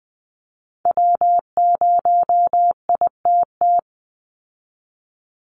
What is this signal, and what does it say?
Morse code, heard as a keyed tone, about 20 wpm, 700 Hz. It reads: W0ITT
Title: Call Signs Repeated 3x - 20wpm